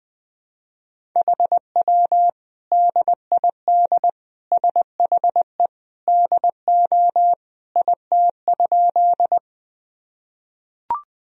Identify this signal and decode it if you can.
Morse code, heard as a keyed tone, about 21 wpm, 700 Hz. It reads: HW DID SHE DO IT?
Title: Sentences from Top 100 Words and Abbreviations 20wpm